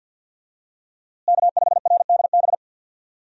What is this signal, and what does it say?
Morse code, heard as a keyed tone, about 52 wpm, 700 Hz. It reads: K5RDB
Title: USA Call Signs - 5 Characters - Repeated 3x - 50wpm